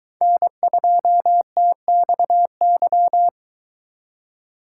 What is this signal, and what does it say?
Morse code, heard as a keyed tone, about 22 wpm, 700 Hz. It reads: N2TXY